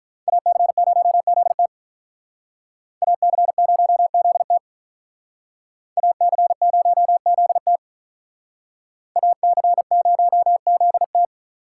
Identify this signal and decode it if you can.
Morse code, heard as a keyed tone, about 42 wpm, 700 Hz. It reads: AC0ZT AC0ZT AC0ZT AC0ZT
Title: Speed Racing - USA 2x2 Call Signs 35wpm